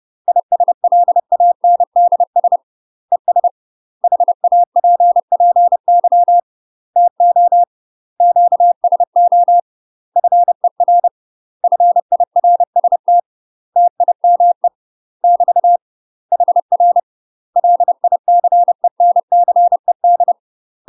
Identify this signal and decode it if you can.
Morse code, heard as a keyed tone, about 30 wpm, 700 Hz. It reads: ISLANDS ES HAPPY TO QSO FER FIRST TIME <BT> HR LICENCED
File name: UK MASTER QSOs TRAINING/30WPM